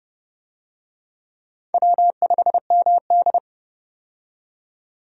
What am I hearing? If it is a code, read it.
Morse code, heard as a keyed tone, about 30 wpm, 700 Hz. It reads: W5MD
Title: USA Call Signs - 4 Characters - 30wpm